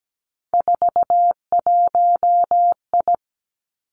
Morse code, heard as a keyed tone, about 17 wpm, 700 Hz. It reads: 41I
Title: Three Challenging Letter-Number Combinations 17wpm